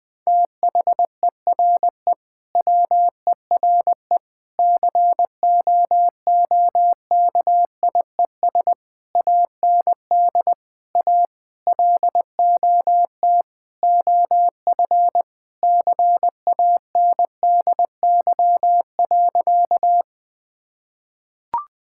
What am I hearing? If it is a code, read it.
Morse code, heard as a keyed tone, about 20 wpm, 700 Hz. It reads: THERE WERE COOKIES AND A LOT OF CANDY.
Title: Sentences with 4-Word Phrases - 20wpm